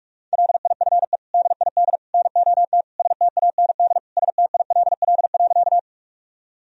Morse code, heard as a keyed tone, about 46 wpm, 700 Hz. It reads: LIFE DID NOT STAND STILL.
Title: Sentences from Top 1000 Words - Review Part 1 - 45wpm